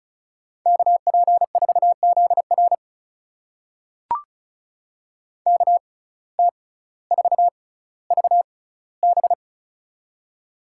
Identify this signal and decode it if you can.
Morse code, heard as a keyed tone, about 36 wpm, spaced as for 18 wpm, 700 Hz. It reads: KP4ZR KT4VB
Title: USA Call Signs - 5 Characters - 6x Character Spacing - 35wpm